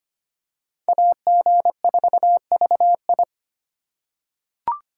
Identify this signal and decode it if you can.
Morse code, heard as a keyed tone, about 25 wpm, 700 Hz. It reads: AG4VI
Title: USA Call Signs - 5 Characters - 2x Character Spacing - 25wpm